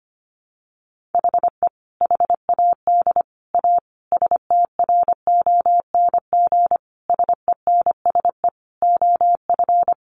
Morse code, heard as a keyed tone, about 25 wpm, 700 Hz. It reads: HE HAD A STRONG SENSE OF
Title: Sentences from Top 1500 Words 25wpm